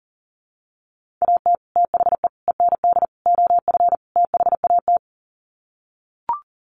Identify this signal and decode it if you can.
Morse code, heard as a keyed tone, about 40 wpm, 700 Hz. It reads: AT THE END OF THAT
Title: Sets of 5 Words Encore2 40wpm